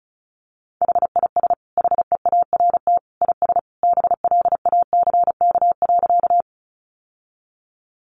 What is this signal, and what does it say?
Morse code, heard as a keyed tone, about 34 wpm, 700 Hz. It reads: HIS HEART IS BLACK.
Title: Sentences from Top 1200 Words 35wpm